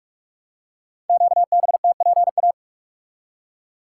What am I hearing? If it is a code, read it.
Morse code, heard as a keyed tone, about 44 wpm, 700 Hz. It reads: QBTPA
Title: Code Groups - 5 Letters - 45wpm